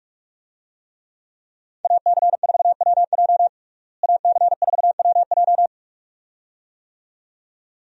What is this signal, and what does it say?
Morse code, heard as a keyed tone, about 44 wpm, 700 Hz. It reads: AC4WJ AC4WJ
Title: USA Call Signs - 5 Characters - Repeated 2x - 45wpm